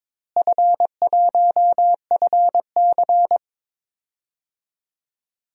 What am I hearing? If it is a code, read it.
Morse code, heard as a keyed tone, about 22 wpm, 700 Hz. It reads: F1FC
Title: DXCC-100 Call Signs - 4 Characters - Repeated 2x - 22wpm